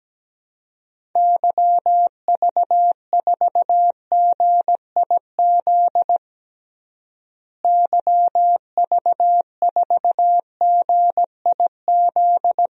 Morse code, heard as a keyed tone, about 17 wpm, 700 Hz. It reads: YV4GIZ YV4GIZ
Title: DXCC-100 Call Signs - 6 Characters - Repeated 2x - 17wpm